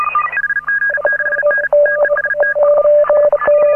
Morse code, heard as a keyed tone, about 37 wpm, 1650 Hz, with noise. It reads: HG6N TEST
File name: AUDIO2018 / HA9A CW